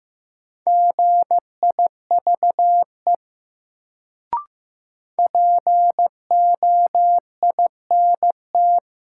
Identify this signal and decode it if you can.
Morse code, heard as a keyed tone, about 15 wpm, 700 Hz. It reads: GIVE POINT
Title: Sentences from Top 200 Words - New Word Drill - 15wpm